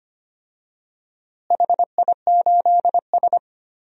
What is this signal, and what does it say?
Morse code, heard as a keyed tone, about 25 wpm, 700 Hz. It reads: HI8S
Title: DXCC-100 Call Signs - 4 Characters - Repeated 3x - 25wpm